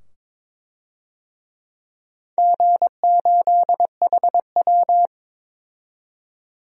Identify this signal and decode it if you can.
Morse code, heard as a keyed tone, about 22 wpm, 700 Hz. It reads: G8HW
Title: DXCC-100 Call Signs - 4 Characters - 22wpm